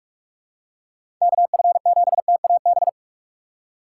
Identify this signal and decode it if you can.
Morse code, heard as a keyed tone, about 46 wpm, 700 Hz. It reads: KF7TAB
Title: USA Call Signs - 4x Character Spacing - 45wpm